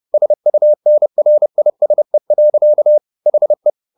Morse code, heard as a keyed tone, about 30 wpm, 600 Hz. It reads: SUNRISE. HE